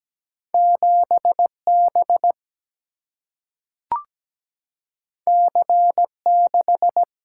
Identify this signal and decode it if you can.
Morse code, heard as a keyed tone, about 17 wpm, 700 Hz. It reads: 7B C6
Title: Single-Letter Single-Number 17wpm